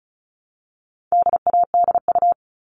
Morse code, heard as a keyed tone, about 34 wpm, 700 Hz. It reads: DADU
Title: All Cut Numbers - 4-Digit 35wpm